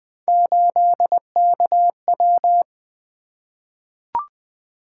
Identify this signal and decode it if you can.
Morse code, heard as a keyed tone, about 20 wpm, 700 Hz. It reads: 8KW
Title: Three Letter-Number Combinations - 20wpm